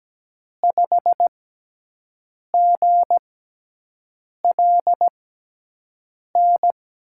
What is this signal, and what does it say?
Morse code, heard as a keyed tone, about 17 wpm, spaced as for 6 wpm, 700 Hz. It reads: 5GLN
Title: USA Call Signs - 5 Characters - 6x Character Spacing - 17wpm